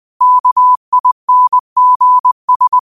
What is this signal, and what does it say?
Morse code, heard as a keyed tone, about 20 wpm, 1000 Hz. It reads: KINGS
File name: qotdfiles / 20wpm